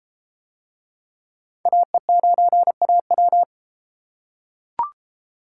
Morse code, heard as a keyed tone, about 31 wpm, 700 Hz. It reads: AE9AW